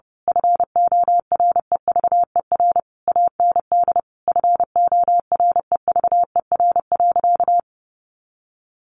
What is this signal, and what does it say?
Morse code, heard as a keyed tone, about 28 wpm, 700 Hz. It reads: FOREVER AND FOREVER.
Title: A Sound Of Thunder 30wpm